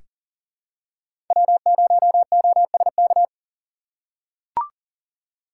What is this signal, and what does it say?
Morse code, heard as a keyed tone, about 40 wpm, 700 Hz. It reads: W0OSK
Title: USA Call Signs - 5 Characters - 3x Character Spacing - 40wpm